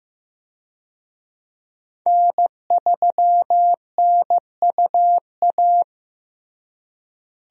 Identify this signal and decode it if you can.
Morse code, heard as a keyed tone, about 15 wpm, 700 Hz. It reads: N3NUA
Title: USA Call Signs - 5 Characters - Repeated 3x - 15wpm